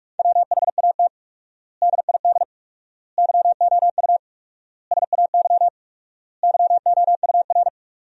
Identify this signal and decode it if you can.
Morse code, heard as a keyed tone, about 46 wpm, 700 Hz. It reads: WHAT DID YOU SAY YOUR
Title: 4x Word-Spacing - Sentences from Top 200 Words - 45wpm